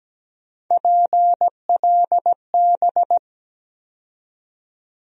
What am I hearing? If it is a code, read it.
Morse code, heard as a keyed tone, about 17 wpm, 700 Hz. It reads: PLB
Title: USA Call Sign Postfixes - 3 Letters 17wpm